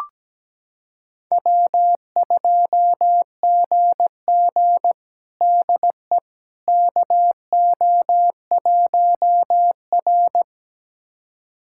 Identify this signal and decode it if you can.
Morse code, heard as a keyed tone, about 17 wpm, 700 Hz. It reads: W2GG DE KO1R
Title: USA Callsign DE Callsign - 4 Characters - 17wpm